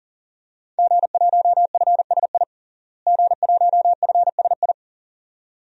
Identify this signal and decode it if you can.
Morse code, heard as a keyed tone, about 40 wpm, 700 Hz. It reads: G1FSI G1FSI
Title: DXCC-100 Call Signs - 5 Characters - Repeated 2x - 40wpm